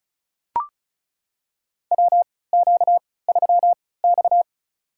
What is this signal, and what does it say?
Morse code, heard as a keyed tone, about 34 wpm, spaced as for 20 wpm, 700 Hz. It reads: WQ3X
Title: USA Call Signs - 4 Characters - 3x Character Spacing - 35wpm